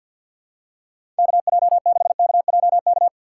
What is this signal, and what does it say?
Morse code, heard as a keyed tone, about 50 wpm, 700 Hz. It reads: KJ6KJK
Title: USA Call Signs - 6 Characters - 3x Character Spacing - 50wpm